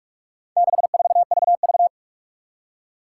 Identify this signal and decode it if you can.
Morse code, heard as a keyed tone, about 46 wpm, 700 Hz. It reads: 64VV
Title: Four Challenging Letter-Number Combinations 45wpm